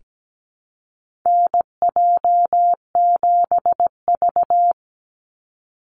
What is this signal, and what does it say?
Morse code, heard as a keyed tone, about 17 wpm, 700 Hz. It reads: NJ7V
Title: NA SOTA Chaser Callsigns 17wpm